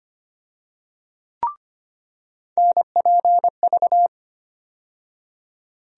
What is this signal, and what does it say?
Morse code, heard as a keyed tone, about 25 wpm, 700 Hz. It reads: NPV
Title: USA Call Sign Postfixes - 3 Letters 25wpm